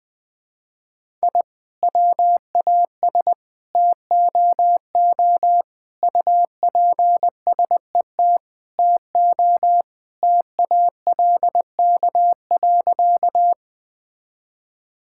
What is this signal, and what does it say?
Morse code, heard as a keyed tone, about 20 wpm, 700 Hz. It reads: I WAS TOO UPSET TO TALK.
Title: Sentences from Top 1000 Words - Review Part 5 - 20wpm